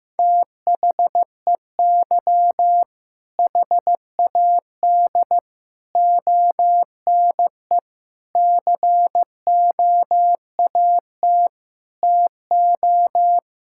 Morse code, heard as a keyed tone, about 16 wpm, 700 Hz. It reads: THEY HAD ONE COAT TO
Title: Sentences from Top 1600 Words 15wpm